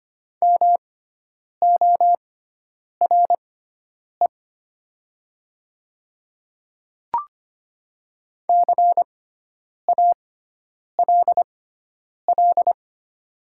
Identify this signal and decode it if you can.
Morse code, heard as a keyed tone, about 25 wpm, spaced as for 9 wpm, 700 Hz. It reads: MORE CALL
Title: 6x Character Spacing - Top 100 Words - 25wpm